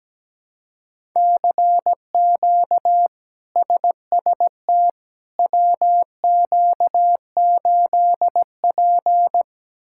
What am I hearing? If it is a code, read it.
Morse code, heard as a keyed tone, about 17 wpm, 700 Hz. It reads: CQ SST WQ8P
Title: K1USN SST QSO Practice - 17wpm